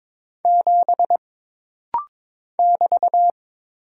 Morse code, heard as a keyed tone, about 22 wpm, 700 Hz. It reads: 7 <BT>
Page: This rapid-fire timing is geared to push your ICR (Instant Character Recognition) skills.